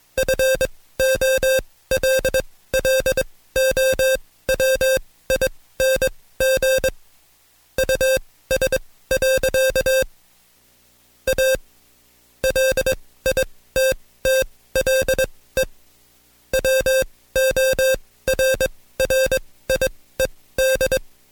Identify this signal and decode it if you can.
Morse code, heard as a keyed tone, about 21 wpm, spaced as for 16 wpm, 530 Hz. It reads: FOLLOWING US. A LITTLE WORRIED
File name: Code Quick / AudioDownld2 / Seg134-Stepper7 15WPM